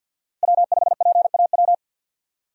Morse code, heard as a keyed tone, about 53 wpm, 700 Hz. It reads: W5PAW
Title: USA Call Signs - 5 Characters - 3x Character Spacing - 50wpm